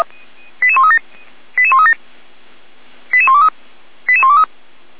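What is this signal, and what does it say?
Five-tone selective call (ZVEI), heard as quick runs of five tones, tones of about 70 ms. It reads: 88137 88137 88123 88123
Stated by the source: ZVEI five-tone selective call 88319, received via SDR.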